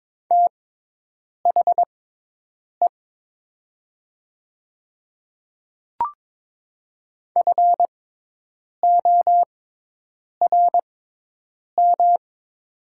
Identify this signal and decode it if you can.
Morse code, heard as a keyed tone, about 22 wpm, spaced as for 8 wpm, 700 Hz. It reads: THE FORM
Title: 6x Character Spacing - Top 200 Words - 22wpm